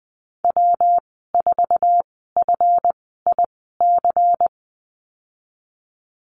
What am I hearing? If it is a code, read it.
Morse code, heard as a keyed tone, about 20 wpm, spaced as for 15 wpm, 700 Hz. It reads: W4FIC
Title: 167 Lesson 27g - Number 7 - Call Signs 20wpm